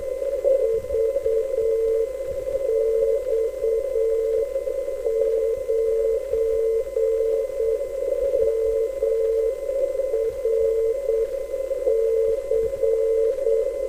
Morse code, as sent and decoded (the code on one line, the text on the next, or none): ...- -..- ----. -- .-. -.-.
VX9MRC